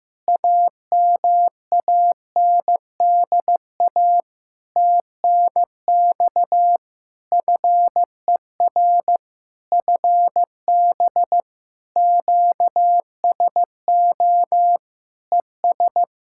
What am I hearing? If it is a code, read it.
Morse code, heard as a keyed tone, about 15 wpm, 700 Hz. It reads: AMANDA TNX FER FB QSO ES
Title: Practice QSO - Exchange 4 - Ending - 15wpm